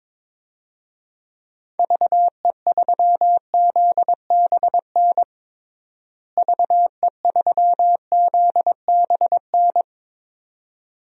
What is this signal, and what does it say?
Morse code, heard as a keyed tone, about 22 wpm, 700 Hz. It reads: VE3ZBN VE3ZBN
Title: DXCC-100 Call Signs - 6 Characters - Repeated 2x - 22wpm